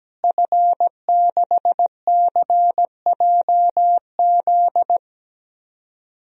Morse code, heard as a keyed tone, about 17 wpm, 700 Hz. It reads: F6CJZ
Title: DXCC-100 Call Signs - 5 Characters - Repeated 3x - 17wpm